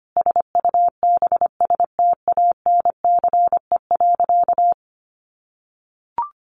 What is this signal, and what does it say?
Morse code, heard as a keyed tone, about 25 wpm, 700 Hz. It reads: SUBSTANCE.
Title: Sentences from Top 1500 Words 25wpm